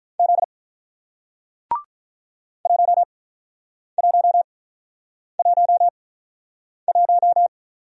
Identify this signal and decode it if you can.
Morse code, heard as a keyed tone, about 47 wpm, 700 Hz. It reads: Z 1 1 1 1